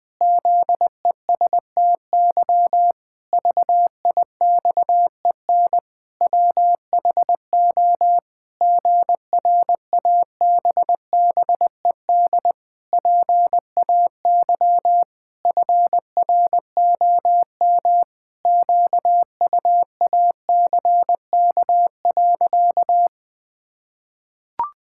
Morse code, heard as a keyed tone, about 20 wpm, 700 Hz. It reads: ZESTY VIXEN WHO GRABBED PAY FROM QUACK.
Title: Pangram Challenge v2 - 20wpm